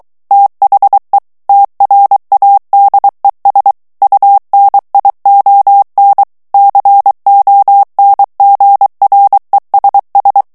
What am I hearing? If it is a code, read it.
Morse code, heard as a keyed tone, about 23 wpm, 800 Hz. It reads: THE TRADES UNION CONGRESS
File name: morse mp3 / ANGLAIS texte clair 1 vitesse 22 mots minute